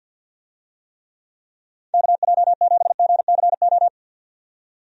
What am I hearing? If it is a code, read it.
Morse code, heard as a keyed tone, about 47 wpm, 700 Hz. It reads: KJ7GCO
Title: USA Call Signs - 6 Characters - 50wpm